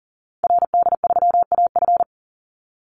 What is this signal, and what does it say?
Morse code, heard as a keyed tone, about 40 wpm, 700 Hz. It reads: RD3AF
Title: Call Signs Repeated 2x - 40wpm